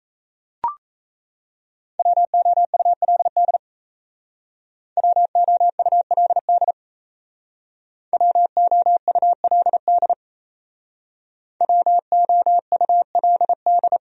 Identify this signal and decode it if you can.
Morse code, heard as a keyed tone, about 35 wpm, 700 Hz. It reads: WOULD WOULD WOULD WOULD